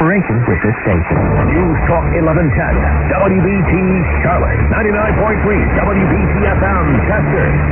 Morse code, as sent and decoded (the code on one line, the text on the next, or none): none